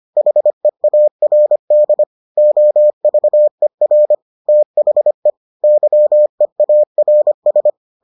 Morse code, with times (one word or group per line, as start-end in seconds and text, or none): 0.2-2.0 HEARD
2.4-4.2 OVER
4.5-5.3 THE
5.6-7.7 YEARS